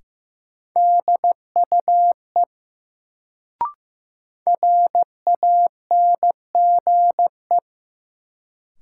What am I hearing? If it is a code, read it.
Morse code, heard as a keyed tone, about 15 wpm, 700 Hz. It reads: DUE RANGE
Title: Sentences from Top 900 Words - New Word Drill - 15wpm